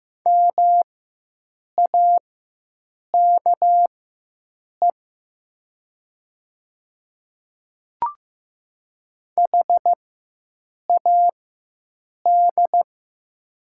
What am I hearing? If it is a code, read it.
Morse code, heard as a keyed tone, about 15 wpm, spaced as for 7 wpm, 700 Hz. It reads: MAKE HAD
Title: 4x Character Spacing - Top 500 Words - 15wpm